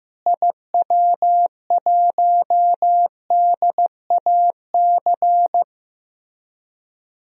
Morse code, heard as a keyed tone, about 16 wpm, 700 Hz. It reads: IW1DAC
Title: DXCC-100 Call Signs - 6 Characters - Repeated 2x - 15wpm